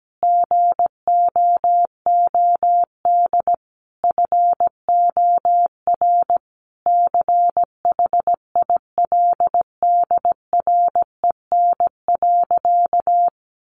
Morse code, heard as a keyed tone, about 17 wpm, 700 Hz. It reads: GOOD FOR CHILDREN.
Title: Sentences from Top 400 Words 17wpm